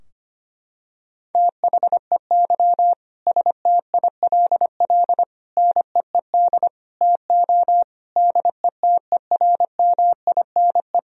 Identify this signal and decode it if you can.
Morse code, heard as a keyed tone, about 25 wpm, 700 Hz. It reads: THEY STILL NEED TO DETERMINE
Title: Sentences from Top 1000 Words - Review Part 4 - 25wpm